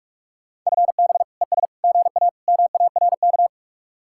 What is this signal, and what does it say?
Morse code, heard as a keyed tone, about 44 wpm, 700 Hz. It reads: FB ES GA MARK